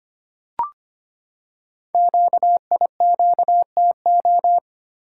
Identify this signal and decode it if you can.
Morse code, heard as a keyed tone, about 25 wpm, 700 Hz. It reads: QIQTO
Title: Code Groups - 5 Letters - 25wpm